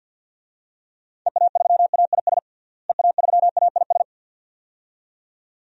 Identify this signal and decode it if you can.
Morse code, heard as a keyed tone, about 50 wpm, 700 Hz. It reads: EA3AIS EA3AIS
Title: DXCC-100 Call Signs - 6 Characters - Repeated 2x - 50wpm